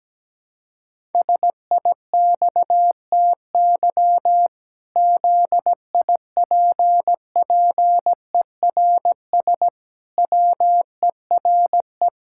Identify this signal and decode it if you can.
Morse code, heard as a keyed tone, about 17 wpm, 700 Hz. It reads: SIXTY ZIPPERS WERE
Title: Pangram Challenge v2 - 17wpm